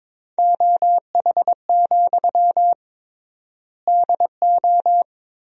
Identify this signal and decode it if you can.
Morse code, heard as a keyed tone, about 22 wpm, 700 Hz. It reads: OH, DO
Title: 3x Word-Spacing - Sentences from Top 500 Words - 22wpm